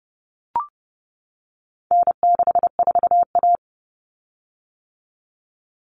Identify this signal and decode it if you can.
Morse code, heard as a keyed tone, about 30 wpm, 700 Hz. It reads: N64A
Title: Common Cut Numbers - 4-Digit 30wpm